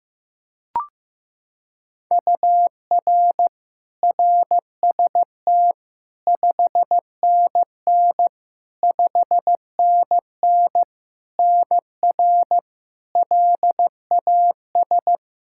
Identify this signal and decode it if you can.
Morse code, heard as a keyed tone, about 15 wpm, 700 Hz. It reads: UR RST 5NN 5NN NR LAS
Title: K1USN SST QSO Practice 15wpm